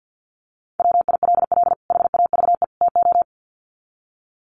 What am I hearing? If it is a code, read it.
Morse code, heard as a keyed tone, about 50 wpm, 700 Hz. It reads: WILL HAVE TO
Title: Sets of 3 Words 50wpm